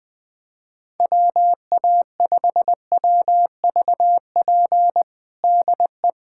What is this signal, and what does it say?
Morse code, heard as a keyed tone, about 21 wpm, 700 Hz. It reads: WA5WVP DE
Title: USA Callsign DE Callsign - 6 Characters - 20wpm